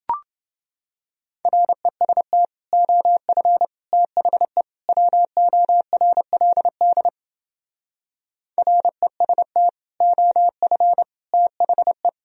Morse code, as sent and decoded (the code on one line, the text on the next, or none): .-. ... - / --- ..-. / - .... / .-- --- .-. .-.. -.. / .-. ... - / --- ..-. / - .... .
REST OF THE WORLD REST OF THE